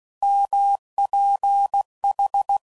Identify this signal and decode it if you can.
Morse code, heard as a keyed tone, about 16 wpm, 800 Hz. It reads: MPH
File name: CW Code Groups @15WPM/Weather Sets